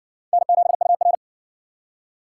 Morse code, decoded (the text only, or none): N7UR